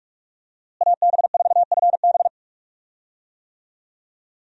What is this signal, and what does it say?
Morse code, heard as a keyed tone, about 45 wpm, 700 Hz. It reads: AB4FB
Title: USA Call Signs - 5 Characters - Repeated 3x - 45wpm